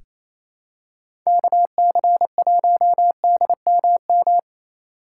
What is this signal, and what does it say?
Morse code, heard as a keyed tone, about 28 wpm, 700 Hz. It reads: KC1DMM
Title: USA Call Signs - 6 Characters - 6x Character Spacing - 28wpm